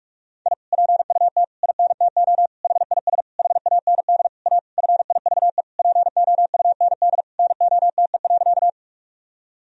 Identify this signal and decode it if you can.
Morse code, heard as a keyed tone, about 41 wpm, 700 Hz. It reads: I PUT INTO HIS HAND A FIVE POUND NOTE.
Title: Sentences from Top 1000 Words - Review Part 2 - 45wpm